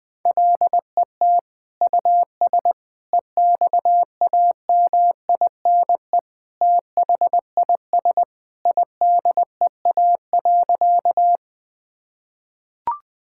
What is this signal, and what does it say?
Morse code, heard as a keyed tone, about 20 wpm, 700 Hz. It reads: LET US EXAMINE THIS IDEA.
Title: Sentences from Top 1600 Words 20wpm